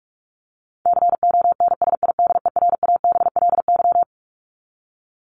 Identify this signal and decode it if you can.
Morse code, heard as a keyed tone, about 44 wpm, 700 Hz. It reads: CONSIDERABLY
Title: LY Suffix - Top 100 Words with Base Words 45wpm